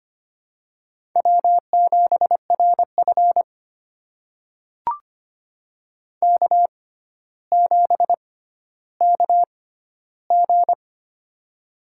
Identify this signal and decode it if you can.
Morse code, heard as a keyed tone, about 25 wpm, spaced as for 13 wpm, 700 Hz. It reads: W7RF K7KG
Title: USA Call Signs - 6x Character Spacing - 25wpm